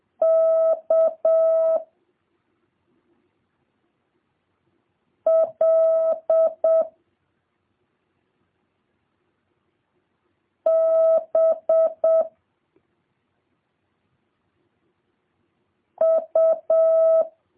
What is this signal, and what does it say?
Morse code, heard as a keyed tone, about 7 wpm, 650 Hz. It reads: K L B U